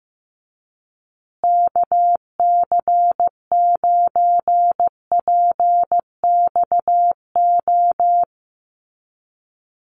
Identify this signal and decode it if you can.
Morse code, heard as a keyed tone, about 15 wpm, 700 Hz. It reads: KC9PXO
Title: USA Call Signs Repeated 3x 15wpm